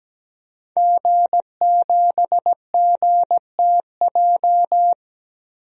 Morse code, heard as a keyed tone, about 17 wpm, 700 Hz. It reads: G7GTJ
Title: DXCC-100 Call Signs - 5 Characters - 17wpm